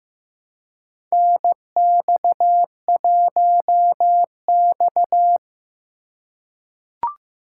Morse code, heard as a keyed tone, about 15 wpm, 700 Hz. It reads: NX1X
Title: USA Call Signs - 4 Characters - 2x Character Spacing - 15wpm